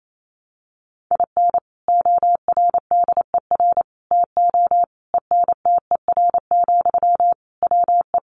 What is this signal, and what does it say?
Morse code, heard as a keyed tone, about 28 wpm, 700 Hz. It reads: IN ORDER TO ENTER, WE
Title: Sentences from Top 1300 Words 28wpm